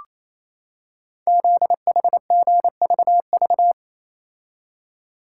Morse code, heard as a keyed tone, about 27 wpm, 700 Hz. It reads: ZHGVV
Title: Code Groups - 5 Letters - 28wpm